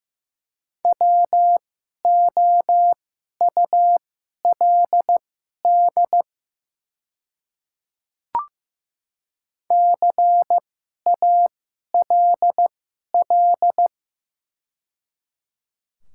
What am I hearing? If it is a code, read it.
Morse code, heard as a keyed tone, about 15 wpm, spaced as for 11 wpm, 700 Hz. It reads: WOULD CALL
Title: 2x Character Spacing - Top 100 Words - 15wpm